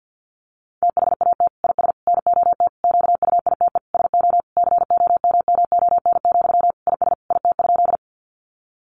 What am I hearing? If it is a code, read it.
Morse code, heard as a keyed tone, about 50 wpm, 700 Hz. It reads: THAT IS NOT QUITE SO COMMON, IS IT?
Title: Sentences from Top 700 Words 50wpm